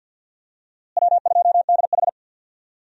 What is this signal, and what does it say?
Morse code, heard as a keyed tone, about 51 wpm, 700 Hz. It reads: W2DH
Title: USA Call Signs - 4 Characters - Repeated 3x - 50wpm